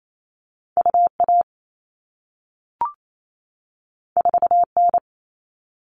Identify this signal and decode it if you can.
Morse code, heard as a keyed tone, about 28 wpm, 700 Hz. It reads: UA 4N
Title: All Cut Numbers - 2-Digit 28wpm